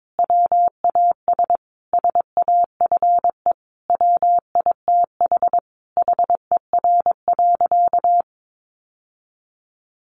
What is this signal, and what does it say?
Morse code, heard as a keyed tone, about 22 wpm, 700 Hz. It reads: WAS SAFE WITH HER.
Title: Sentences from Top 900 Words 22wpm